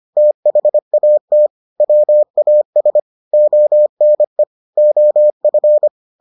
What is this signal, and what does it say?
Morse code, heard as a keyed tone, about 25 wpm, 600 Hz. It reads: THAT WAS ONE OF